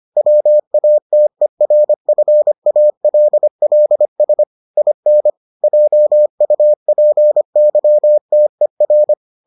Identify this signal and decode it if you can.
Morse code, heard as a keyed tone, about 25 wpm, 600 Hz. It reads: WATERFALLS IN JUPYTER